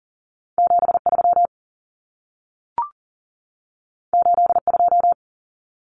Morse code, heard as a keyed tone, about 40 wpm, 700 Hz. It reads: 73 82
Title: Common Cut Numbers - 2-Digit 40wpm